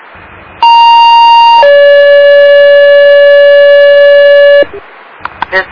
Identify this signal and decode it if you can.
Two-tone page, heard as a one-second tone, then a three-second tone, about 930 then 570 Hz, with noise. It is heard from 0.6 to 4.6 s.